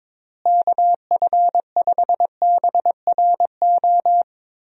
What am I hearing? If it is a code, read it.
Morse code, heard as a keyed tone, about 22 wpm, 700 Hz. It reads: KF5BRO
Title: USA Call Signs - 6 Characters - Repeated 3x - 22wpm